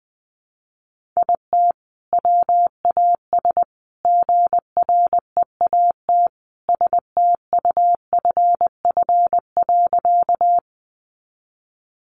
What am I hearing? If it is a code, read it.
Morse code, heard as a keyed tone, about 20 wpm, 700 Hz. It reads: IT WAS GREAT STUFF.
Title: Sentences from Top 1400 Words 20wpm